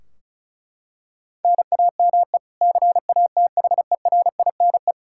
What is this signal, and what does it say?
Morse code, heard as a keyed tone, about 34 wpm, 700 Hz. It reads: NAME CATHERINE
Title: Practice QSO - Exchange 1 - Beginning - 35wpm